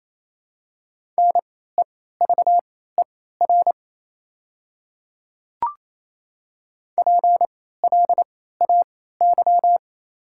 Morse code, heard as a keyed tone, about 28 wpm, spaced as for 16 wpm, 700 Hz. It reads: NEVER PLAY
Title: 3x Character Spacing - Top 300 Words - 28wpm